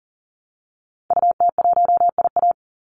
Morse code, heard as a keyed tone, about 40 wpm, 700 Hz. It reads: UT1IA
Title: Call Signs Repeated 3x - 40wpm